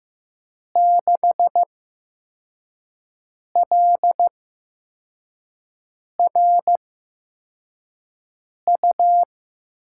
Morse code, heard as a keyed tone, about 15 wpm, spaced as for 4 wpm, 700 Hz. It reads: 6LRU
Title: USA Call Signs - 8x Character Spacing - 15wpm